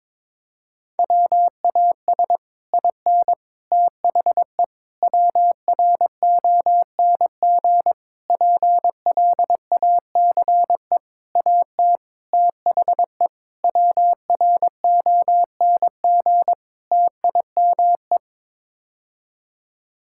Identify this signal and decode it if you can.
Morse code, heard as a keyed tone, about 22 wpm, 700 Hz. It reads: WAS IN THE WRONG PLACE AT THE WRONG TIME
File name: Sets of 9 Words - 22wpm